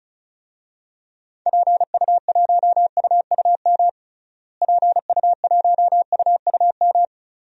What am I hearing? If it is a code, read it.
Morse code, heard as a keyed tone, about 34 wpm, 700 Hz. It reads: PU1UUM PU1UUM
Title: DXCC-100 Call Signs - 6 Characters - Repeated 2x - 35wpm